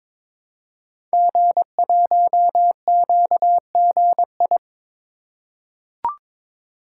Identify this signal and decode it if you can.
Morse code, heard as a keyed tone, about 22 wpm, 700 Hz. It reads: G1QGI
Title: Code Groups - 5 Letter:Numbers - 22wpm